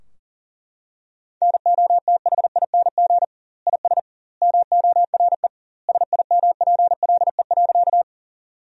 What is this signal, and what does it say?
Morse code, heard as a keyed tone, about 40 wpm, 700 Hz. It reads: NOTHING IS MORE SIMPLE.
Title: 2x Word-Spacing - Sentences from Top 500 Words - 40wpm